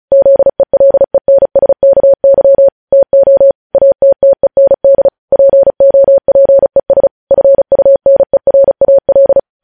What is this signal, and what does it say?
Morse code, heard as a keyed tone, about 36 wpm, 550 Hz. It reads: ZELENSKY TO ATTEND POPES FUNERAL
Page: news headlines in morse code at 35 wpm